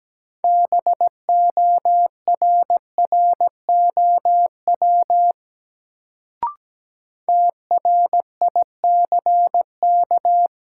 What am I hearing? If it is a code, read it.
Morse code, heard as a keyed tone, about 17 wpm, 700 Hz. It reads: BORROW TRICK
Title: Sentences from Top 1500 Words - New Word Drill Alt - 17wpm